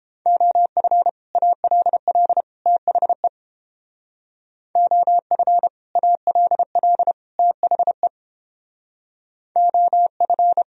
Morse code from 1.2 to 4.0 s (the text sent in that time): ALL THE